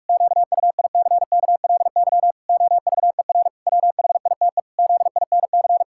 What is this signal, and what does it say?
Morse code, heard as a keyed tone, about 43 wpm, 700 Hz. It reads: QUICKLY OVER WHITE ZINC
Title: Pangram Challenge v2 - 45wpm